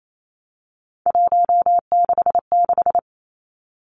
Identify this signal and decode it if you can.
Morse code, heard as a keyed tone, about 28 wpm, 700 Hz. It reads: 166
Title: Three Challenging Letter-Number Combinations 28wpm